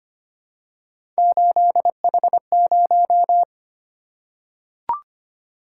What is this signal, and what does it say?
Morse code, heard as a keyed tone, about 25 wpm, 700 Hz. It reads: 8H0
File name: Three Letter-Number Combinations - 25wpm